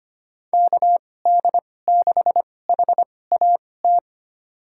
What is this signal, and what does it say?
Morse code, heard as a keyed tone, about 25 wpm, spaced as for 18 wpm, 700 Hz. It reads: KD6HAT
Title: USA Call Signs - 2x Character Spacing - 25wpm